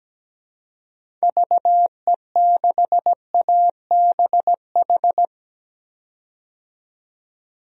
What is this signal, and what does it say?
Morse code, heard as a keyed tone, about 17 wpm, 700 Hz. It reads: VE6ABH
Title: DXCC-100 Call Signs - 6 Characters - 17wpm